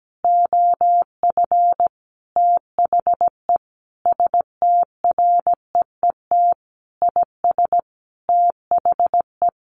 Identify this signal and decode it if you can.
Morse code, heard as a keyed tone, about 17 wpm, 700 Hz. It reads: OF THE STREET IS THE
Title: Sentences from Top 1400 Words 17wpm